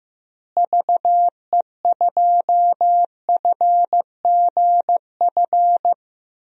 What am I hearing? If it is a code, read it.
Morse code, heard as a keyed tone, about 15 wpm, 700 Hz. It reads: VE2FGF
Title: DXCC-100 Call Signs - 6 Characters - Repeated 2x - 15wpm